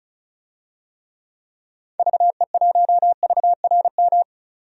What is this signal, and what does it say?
Morse code, heard as a keyed tone, about 36 wpm, 700 Hz. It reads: VE1VRM
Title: DXCC-100 Call Signs - 6 Characters - 35wpm